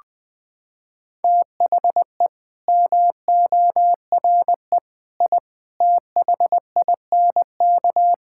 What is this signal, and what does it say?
Morse code, heard as a keyed tone, about 20 wpm, 700 Hz. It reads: THE MORE I THINK
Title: Sentences with 9-Word Phrases - 20wpm